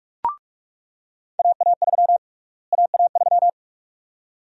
Morse code, heard as a keyed tone, about 41 wpm, 700 Hz. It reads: AA3 AA3
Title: USA Call Sign Prefixes - Repeated 2x - 45wpm